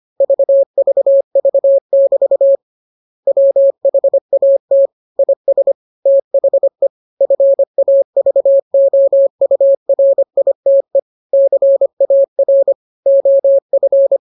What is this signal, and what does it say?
Morse code, heard as a keyed tone, about 25 wpm, 550 Hz. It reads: VVV<BT> WHAT IS THE FAVOURITE CAR OF